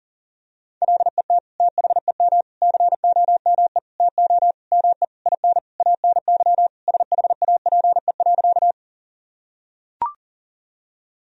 Morse code, heard as a keyed tone, about 40 wpm, 700 Hz. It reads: LET THEM COME TO ME IN ANY SHAPE.
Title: Sentences from Top 1000 Words - Review Part 3 - 40wpm